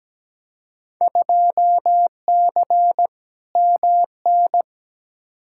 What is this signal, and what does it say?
Morse code, heard as a keyed tone, about 17 wpm, 700 Hz. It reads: 2C MN
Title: Field Day Exchanges Encore - 17wpm